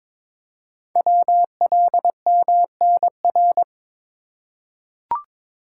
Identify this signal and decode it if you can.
Morse code, heard as a keyed tone, about 23 wpm, 700 Hz. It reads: WLMNR
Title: Code Groups - 5 Letters - 22wpm